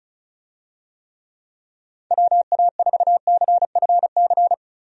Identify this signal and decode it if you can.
Morse code, heard as a keyed tone, about 34 wpm, 700 Hz. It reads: WA4CFC
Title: USA Call Signs - 6 Characters - 35wpm